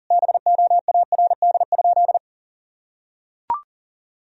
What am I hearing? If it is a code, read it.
Morse code, heard as a keyed tone, about 37 wpm, 700 Hz. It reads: BOARD?
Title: Sentences from Top 1000 Words - Review Part 4 - 40wpm